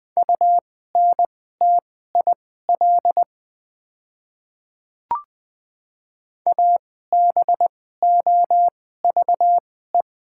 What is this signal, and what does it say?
Morse code, heard as a keyed tone, about 20 wpm, spaced as for 14 wpm, 700 Hz. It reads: UNTIL ABOVE
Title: 2x Character Spacing - Top 500 Words - 20wpm